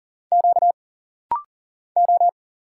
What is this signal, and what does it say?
Morse code, heard as a keyed tone, about 40 wpm, 700 Hz. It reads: Q O
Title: Single Letters - Rapid-Fire 40wpm